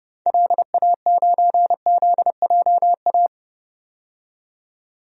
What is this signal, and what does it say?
Morse code, heard as a keyed tone, about 29 wpm, 700 Hz. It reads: LA9ZJA
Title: DXCC-100 Call Signs - 6 Characters - 30wpm